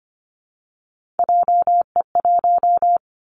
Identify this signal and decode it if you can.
Morse code, heard as a keyed tone, about 25 wpm, 700 Hz. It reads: JE1
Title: Callsign Prefixes for Top 25 DXCC - 25wpm